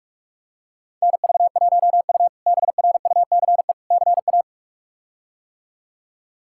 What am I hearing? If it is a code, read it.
Morse code, heard as a keyed tone, about 46 wpm, 700 Hz. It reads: NV1U BRUCE CA
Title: K1USN SST QSO Practice - 45wpm